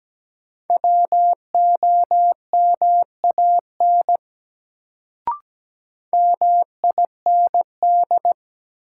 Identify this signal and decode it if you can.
Morse code, heard as a keyed tone, about 17 wpm, 700 Hz. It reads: WOMAN MIND
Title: Sets of 5 Words - New Word Drill - 17wpm